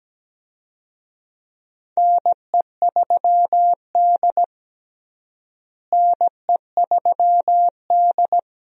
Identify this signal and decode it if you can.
Morse code, heard as a keyed tone, about 17 wpm, 700 Hz. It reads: NE3D NE3D
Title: USA Call Signs - 4 Characters - Repeated 2x - 17wpm